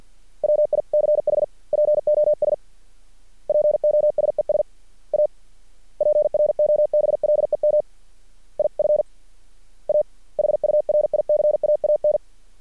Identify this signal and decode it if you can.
Morse code, heard as a keyed tone, about 50 wpm, 600 Hz. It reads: WICH POS POSES A PROBLEM IF A HURICAAN
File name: archived / CRSnet